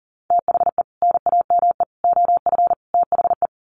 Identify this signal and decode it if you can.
Morse code, heard as a keyed tone, about 40 wpm, 700 Hz. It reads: THE NAME OF THE
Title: Sets of 4 Words 40wpm